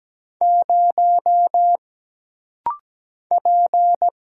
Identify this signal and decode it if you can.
Morse code, heard as a keyed tone, about 17 wpm, 700 Hz. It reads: 0 P
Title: Single Letter-Number - Rapid-Fire 17wpm